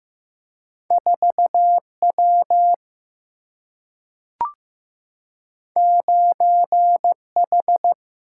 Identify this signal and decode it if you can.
Morse code, heard as a keyed tone, about 15 wpm, 700 Hz. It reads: 4 W 9 H
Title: Single-Letter Single-Number 15wpm